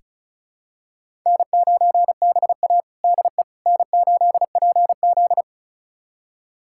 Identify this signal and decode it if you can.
Morse code, heard as a keyed tone, about 34 wpm, 700 Hz. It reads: N9BA DE N8PZ
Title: USA Callsign DE Callsign - 4 Characters - 35wpm